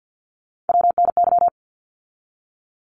Morse code, heard as a keyed tone, about 50 wpm, 700 Hz. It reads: WNY
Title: ARRL Sections 50wpm